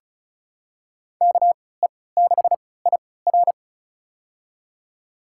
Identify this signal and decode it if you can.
Morse code, heard as a keyed tone, about 36 wpm, spaced as for 20 wpm, 700 Hz. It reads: KE6IR
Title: USA Call Signs - 5 Characters - 3x Character Spacing - 35wpm